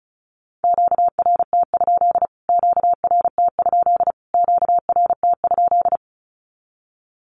Morse code, heard as a keyed tone, about 34 wpm, 700 Hz. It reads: QRT? QRT? QRT?
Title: Q-Codes Repeated 3x 35wpm